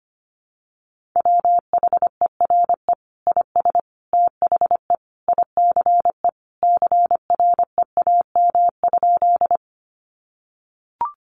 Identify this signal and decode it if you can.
Morse code, heard as a keyed tone, about 25 wpm, 700 Hz. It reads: WHERE IS THE ICE CREAM?
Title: Sentences from Top 1500 Words 25wpm